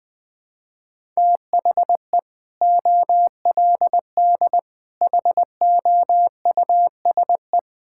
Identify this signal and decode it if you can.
Morse code, heard as a keyed tone, about 20 wpm, 700 Hz. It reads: THE OLD HOUSE
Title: Sentences with 2-Word Phrases - 20wpm